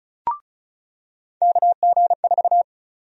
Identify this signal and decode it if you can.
Morse code, heard as a keyed tone, about 34 wpm, 700 Hz. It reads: KG4
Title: USA Call Sign Prefixes - 35wpm